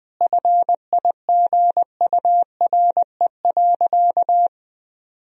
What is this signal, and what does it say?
Morse code, heard as a keyed tone, about 20 wpm, 700 Hz. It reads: FIGURE.
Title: Sentences from Top 1000 Words - Review Part 4 - 20wpm